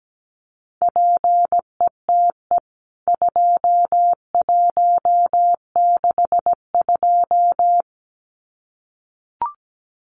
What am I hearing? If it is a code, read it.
Morse code, heard as a keyed tone, about 17 wpm, 700 Hz. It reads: PETE 2162
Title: CWT Exchange 17wpm